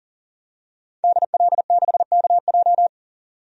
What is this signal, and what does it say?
Morse code, heard as a keyed tone, about 40 wpm, 700 Hz. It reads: DL6KJ
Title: DXCC-100 Call Signs - 5 Characters - 40wpm